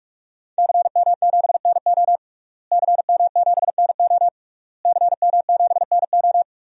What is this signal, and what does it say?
Morse code, heard as a keyed tone, about 47 wpm, 700 Hz. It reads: CM7NO CM7NO CM7NO
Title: DXCC-100 Call Signs - 5 Characters - Repeated 3x - 45wpm